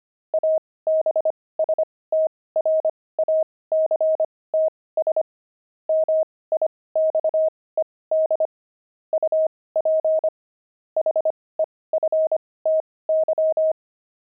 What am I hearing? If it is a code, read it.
Morse code, heard as a keyed tone, about 25 wpm, spaced as for 18 wpm, 630 Hz. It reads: ABSTRACTS MIXED UP HEFTY
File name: pangram-fs-18-cs-25-jocks